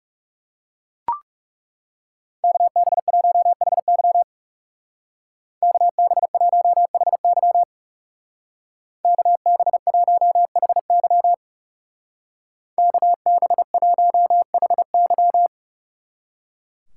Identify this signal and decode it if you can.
Morse code, heard as a keyed tone, about 37 wpm, 700 Hz. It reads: KB1HY KB1HY KB1HY KB1HY
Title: Speed Racing - USA 2x2 Call Signs 30wpm